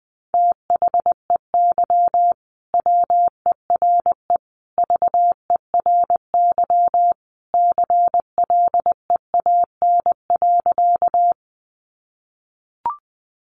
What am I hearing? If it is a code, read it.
Morse code, heard as a keyed tone, about 20 wpm, 700 Hz. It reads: THEY WERE VERY CLEAN.
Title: Sentences from Top 1100 Words 20wpm